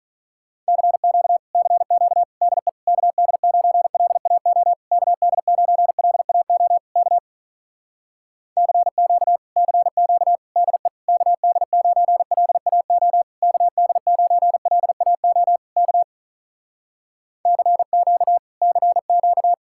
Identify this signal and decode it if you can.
Morse code, heard as a keyed tone, about 48 wpm, 700 Hz. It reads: CQ CQ DE KD9LAO KD9LAO K CQ CQ DE KD9LAO KD9LAO K CQ CQ